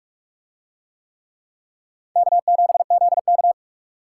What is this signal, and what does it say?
Morse code, heard as a keyed tone, about 42 wpm, 700 Hz. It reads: K7ZK